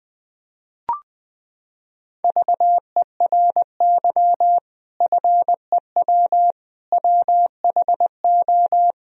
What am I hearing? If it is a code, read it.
Morse code, heard as a keyed tone, about 20 wpm, 700 Hz. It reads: VERY FEW WHO
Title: Sentences from Top 1000 Words - Review Part 2 - 20wpm